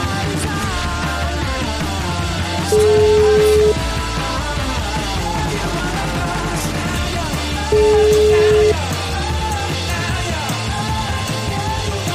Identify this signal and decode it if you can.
Ringback tone (Europe, 425 Hz), with noise.